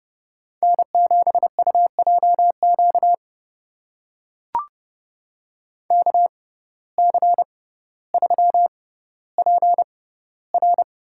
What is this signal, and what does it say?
Morse code, heard as a keyed tone, about 31 wpm, spaced as for 15 wpm, 700 Hz. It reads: N7UJQ KC3PR
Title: USA Call Signs - 5 Characters - 6x Character Spacing - 30wpm